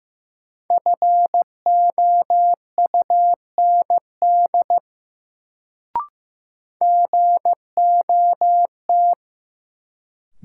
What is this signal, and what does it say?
Morse code, heard as a keyed tone, about 15 wpm, 700 Hz. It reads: FOUND GOT
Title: Sentences from Top 300 Words - New Word Drill - 15wpm